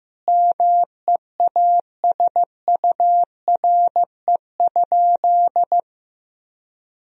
Morse code, heard as a keyed tone, about 15 wpm, 700 Hz. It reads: MEASURE?
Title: Sentences from Top 1000 Words - Review Part 2 - 15wpm